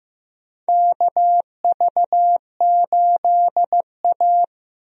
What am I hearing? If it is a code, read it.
Morse code, heard as a keyed tone, about 15 wpm, 700 Hz. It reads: KV8A
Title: USA Call Signs - 4 Characters - 15wpm